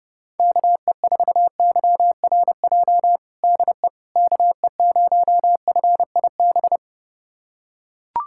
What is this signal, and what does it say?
Morse code, heard as a keyed tone, about 30 wpm, 700 Hz. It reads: KE4YRJ DE KE0FIB
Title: USA Callsign DE Callsign - 6 Characters - 30wpm